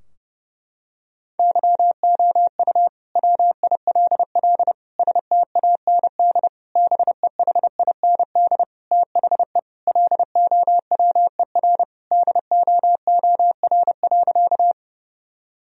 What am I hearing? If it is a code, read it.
Morse code, heard as a keyed tone, about 30 wpm, 700 Hz. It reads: YOU WILL STAND BEHIND THE LOWER DOOR.
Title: Sentences from Top 1000 Words - Review Part 4 - 30wpm